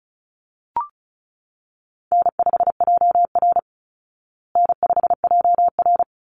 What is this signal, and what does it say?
Morse code, heard as a keyed tone, about 34 wpm, 700 Hz. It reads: N5JR N5JR
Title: Call Signs Repeated 3x - 35wpm